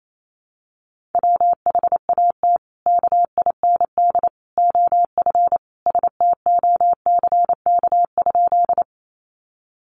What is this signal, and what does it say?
Morse code, heard as a keyed tone, about 28 wpm, 700 Hz. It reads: WHAT KIND OF STOCK?
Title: Sentences from Top 800 Words 28wpm